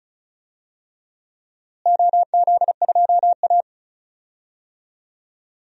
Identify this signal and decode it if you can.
Morse code, heard as a keyed tone, about 34 wpm, 700 Hz. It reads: OZ2A
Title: DXCC-100 Call Signs - 4 Characters - Repeated 3x - 35wpm